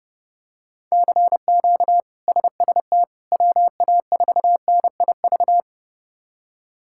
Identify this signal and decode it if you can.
Morse code, heard as a keyed tone, about 30 wpm, 700 Hz. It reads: CQ SST WA4NIV
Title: K1USN SST QSO Practice - 30wpm